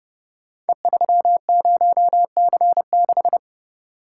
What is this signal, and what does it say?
Morse code, heard as a keyed tone, about 29 wpm, 700 Hz. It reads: E30C6
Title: Code Groups - 5 Letter:Numbers - 30wpm